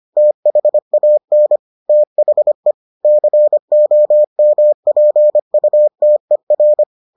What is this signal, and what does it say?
Morse code, heard as a keyed tone, about 25 wpm, 600 Hz. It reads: THAN THE COMPUTER